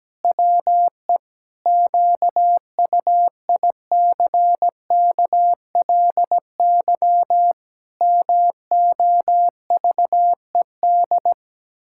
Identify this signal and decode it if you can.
Morse code, heard as a keyed tone, about 17 wpm, 700 Hz. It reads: WE QUICKLY MOVED
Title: Pangram Challenge v2 - 17wpm